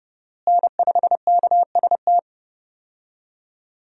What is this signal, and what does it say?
Morse code, heard as a keyed tone, about 30 wpm, 700 Hz. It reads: N5KST
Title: USA Call Signs - 5 Characters - Repeated 2x - 30wpm